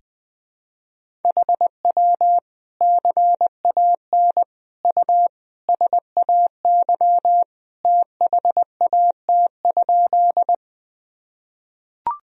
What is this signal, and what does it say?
Morse code, heard as a keyed tone, about 21 wpm, 700 Hz. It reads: HW CAN U SAY THAT?
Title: Sentences from Top 200 Words and Abbreviations 20wpm